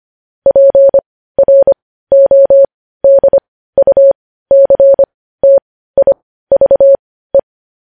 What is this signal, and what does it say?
Morse code, heard as a keyed tone, about 25 wpm, spaced as for 15 wpm, 550 Hz. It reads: PRODUCTIVE
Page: news-headlines-in-morse-code-at-15-wpm